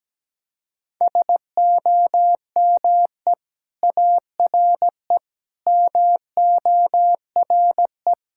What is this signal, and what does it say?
Morse code, heard as a keyed tone, about 17 wpm, 700 Hz. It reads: SOME ARE MORE
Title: Sentences from Top 1000 Words - Review Part 6 - 17wpm